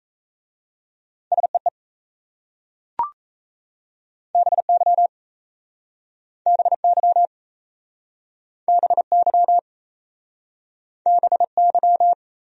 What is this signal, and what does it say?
Morse code, heard as a keyed tone, about 36 wpm, 700 Hz. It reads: SEE BY BY BY BY